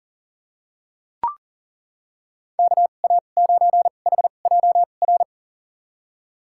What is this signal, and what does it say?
Morse code, heard as a keyed tone, about 40 wpm, spaced as for 29 wpm, 700 Hz. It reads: KA9HJR
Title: USA Call Signs - 6 Characters - 2x Character Spacing - 40wpm